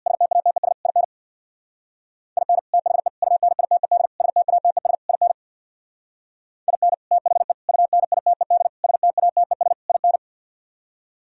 Morse code, heard as a keyed tone, about 64 wpm, 700 Hz. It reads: STATES IN IN THE UNITED STATES IN IN THE UNITED STATES IN